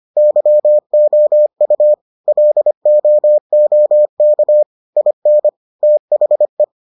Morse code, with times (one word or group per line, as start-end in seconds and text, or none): 0.2-1.9 YOU
2.3-4.6 LOOK
5.0-5.5 IN
5.8-6.6 THE